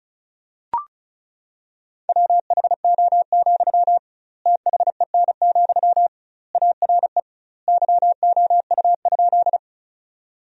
Morse code, heard as a keyed tone, about 34 wpm, 700 Hz. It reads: WHO, THEN, ARE YOU?
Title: 2x Word-Spacing - Sentences from Top 100 Words - 35wpm